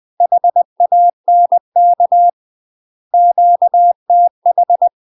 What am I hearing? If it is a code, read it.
Morse code, heard as a keyed tone, about 20 wpm, 700 Hz. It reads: HANK QTH
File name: UK MASTER QSOs TRAINING / 20WPM / UK MASTER QSOs 0024